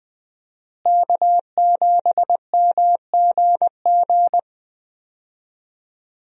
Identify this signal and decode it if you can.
Morse code, heard as a keyed tone, about 20 wpm, 700 Hz. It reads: K7MGG
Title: USA Call Signs - 5 Characters - Repeated 3x - 20wpm